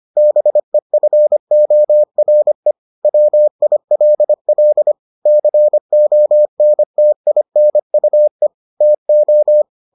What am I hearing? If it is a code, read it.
Morse code, heard as a keyed tone, about 25 wpm, 600 Hz. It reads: BEFORE WILL CONTINUE TO